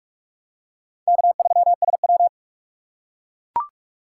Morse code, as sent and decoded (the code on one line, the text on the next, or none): -.- ...-- ... .--
K3SW